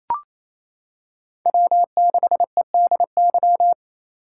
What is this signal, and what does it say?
Morse code, heard as a keyed tone, about 27 wpm, 700 Hz. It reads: W6EDY
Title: K1USN SST QSO Practice - 28wpm